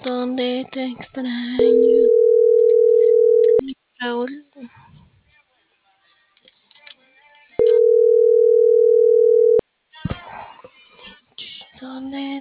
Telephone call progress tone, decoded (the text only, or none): ringback tone